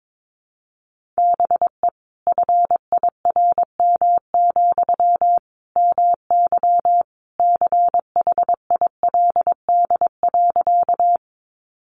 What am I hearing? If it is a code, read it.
Morse code, heard as a keyed tone, about 22 wpm, 700 Hz. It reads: BE FIRM, MY CHILD.
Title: Sentences from Top 1400 Words 22wpm